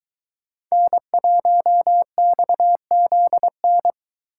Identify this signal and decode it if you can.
Morse code, heard as a keyed tone, about 22 wpm, 700 Hz. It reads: N1XZN